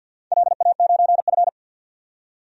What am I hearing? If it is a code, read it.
Morse code, heard as a keyed tone, about 53 wpm, 700 Hz. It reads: LA9F
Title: DXCC-100 Call Signs - 4 Characters - 50wpm